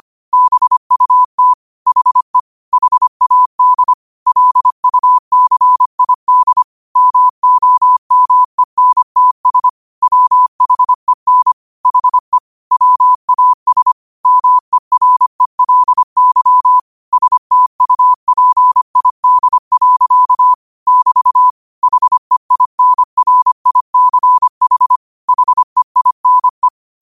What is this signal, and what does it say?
Morse code, heard as a keyed tone, about 25 wpm, 1000 Hz. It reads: BUT HE HAD LUCID MOMENTS WHEN HE WAS MERELY STUPID. <BT> HEINRICH HEINE
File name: qotdfiles / 25wpm